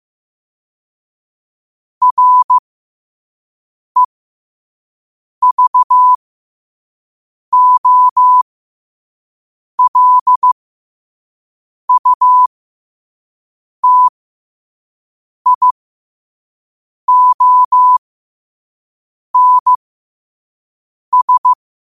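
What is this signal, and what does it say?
Morse code, heard as a keyed tone, about 15 wpm, spaced as for 5 wpm, 1000 Hz. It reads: REVOLUTIONS
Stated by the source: Morse sent at 5 wpm.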